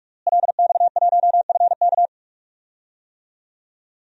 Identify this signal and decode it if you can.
Morse code, heard as a keyed tone, about 41 wpm, 700 Hz. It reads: LX1FK
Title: DXCC-100 Call Signs - 5 Characters - Repeated 2x - 45wpm